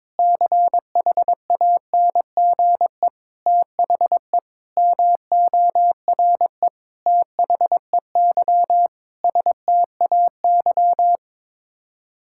Morse code, heard as a keyed tone, about 22 wpm, 700 Hz. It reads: CHANGE THE MORE THEY STAY
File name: practice / 22wpm / Sets of 8 Words - 22wpm